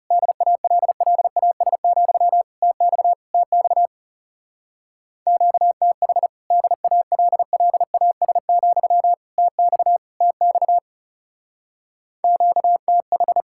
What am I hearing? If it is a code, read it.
Morse code, heard as a keyed tone, about 36 wpm, 700 Hz. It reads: DALLAS, TX TX QTH DALLAS, TX TX QTH
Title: Speed-Racing Common QSO Phrases Encore - 30wpm